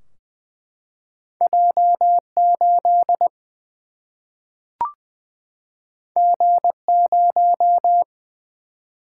Morse code, heard as a keyed tone, about 20 wpm, 700 Hz. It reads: J 8 G 0
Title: Single-Letter Single-Number 20wpm